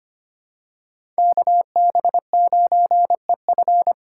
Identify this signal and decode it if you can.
Morse code, heard as a keyed tone, about 25 wpm, 700 Hz. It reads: KB9EF
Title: USA Call Signs - 3x Character Spacing - 25wpm